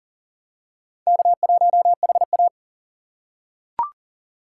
Morse code, heard as a keyed tone, about 37 wpm, 700 Hz. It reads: K1HA
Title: USA Call Signs - 4 Characters - 6x Character Spacing - 40wpm